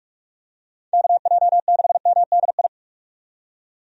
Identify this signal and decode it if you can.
Morse code, heard as a keyed tone, about 44 wpm, 700 Hz. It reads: KJ6MDI
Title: USA Call Signs - 6 Characters - 45wpm